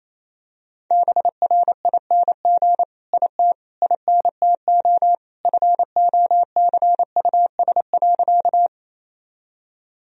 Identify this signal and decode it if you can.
Morse code, heard as a keyed tone, about 28 wpm, 700 Hz. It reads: BRING IT INTO FOCUS.
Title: Sentences from Top 1000 Words - Review Part 4 - 28wpm